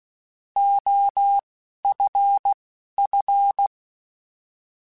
Morse code, heard as a keyed tone, about 16 wpm, spaced as for 11 wpm, 800 Hz. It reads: OFF
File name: Alphabetical / Alphabetical mp3s / Os @10